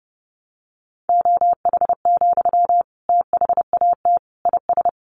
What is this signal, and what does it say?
Morse code, heard as a keyed tone, about 28 wpm, 700 Hz. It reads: OH, THAT IS
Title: Sentences from Top 500 Words 30wpm